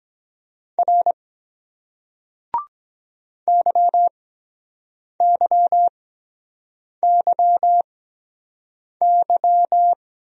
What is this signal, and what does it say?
Morse code, heard as a keyed tone, about 23 wpm, spaced as for 7 wpm, 700 Hz. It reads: R YYYY